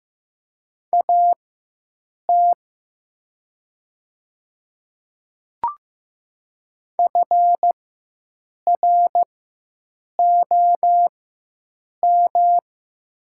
Morse code, heard as a keyed tone, about 15 wpm, spaced as for 7 wpm, 700 Hz. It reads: AT FROM
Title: 4x Character Spacing - Top 200 Words - 15wpm